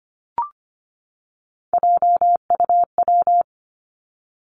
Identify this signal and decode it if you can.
Morse code, heard as a keyed tone, about 25 wpm, 700 Hz. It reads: JUW
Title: Three Challenging Letter-Number Combinations 25wpm